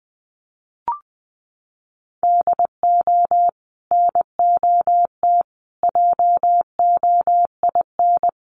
Morse code, heard as a keyed tone, about 20 wpm, 700 Hz. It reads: DO NOT JOIN
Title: Sentences from Top 1400 Words 20wpm